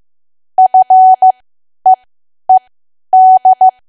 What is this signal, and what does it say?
Morse code, heard as a keyed tone, about 15 wpm, spaced as for 10 wpm, 750 Hz. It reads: FEED
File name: Archive / 10 WPM / 10WPM